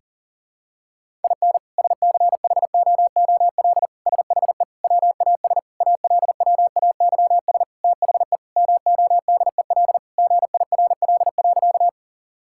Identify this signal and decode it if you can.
Morse code, heard as a keyed tone, about 40 wpm, 700 Hz. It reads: IN SCHOOL SHE WAS ALWAYS THE MODEL GIRL.
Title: Sentences from Top 1000 Words - Review Part 5 - 40wpm